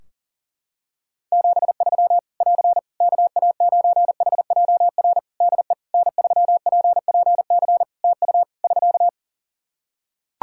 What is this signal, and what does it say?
Morse code, heard as a keyed tone, about 40 wpm, 700 Hz. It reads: 73 <AR> KA9HJR DE N3PPC TU <SK>
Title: Practice QSO - Exchange 4 - Ending - 40wpm